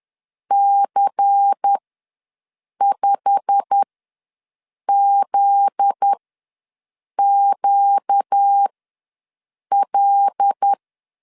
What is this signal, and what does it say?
Morse code, heard as a keyed tone, about 10 wpm, spaced as for 6 wpm, 800 Hz. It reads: C5ZQL
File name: LEZIONI / 8 YJ45 / 8 YJ45 ws 10wpm 3843